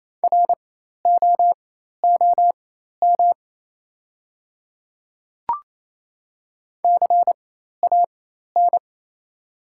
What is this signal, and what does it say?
Morse code, heard as a keyed tone, about 27 wpm, spaced as for 13 wpm, 700 Hz. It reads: ROOM CAN
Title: 4x Character Spacing - Top 300 Words - 28wpm